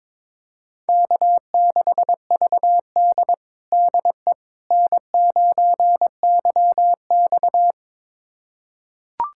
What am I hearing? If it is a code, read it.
Morse code, heard as a keyed tone, about 22 wpm, 700 Hz. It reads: K6VD DE N9YX
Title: USA Callsign DE Callsign - 4 Characters - 22wpm